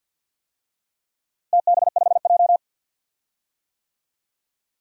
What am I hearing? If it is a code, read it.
Morse code, heard as a keyed tone, about 51 wpm, 700 Hz. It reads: TB5J
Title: DXCC-100 Call Signs - 4 Characters - Repeated 2x - 50wpm